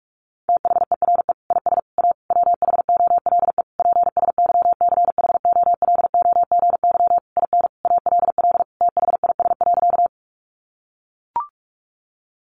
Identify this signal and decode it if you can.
Morse code, heard as a keyed tone, about 44 wpm, 700 Hz. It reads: THERE IS A WHOLE PSYCHOLOGY IN ALL THIS.
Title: Sentences from Top 1400 Words 45wpm